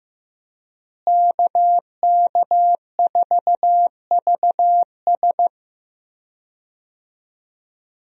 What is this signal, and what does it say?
Morse code, heard as a keyed tone, about 15 wpm, 700 Hz. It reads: KK4VS
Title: USA Call Signs - 5 Characters - Repeated 3x - 15wpm